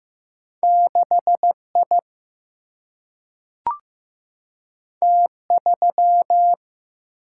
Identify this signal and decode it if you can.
Morse code, heard as a keyed tone, about 15 wpm, 700 Hz. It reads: 6I T3
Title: Single-Letter Single-Number 15wpm